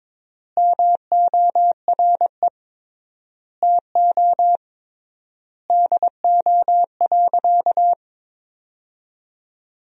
Morse code, heard as a keyed tone, about 22 wpm, 700 Hz. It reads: MORE TO DO.
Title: 3x Word-Spacing - Sentences from Top 100 Words - 22wpm